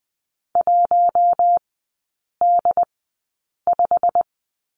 Morse code, heard as a keyed tone, about 19 wpm, spaced as for 8 wpm, 700 Hz. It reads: 1D5
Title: 100 Lesson 17e - Letter u - Three Characters 20wpm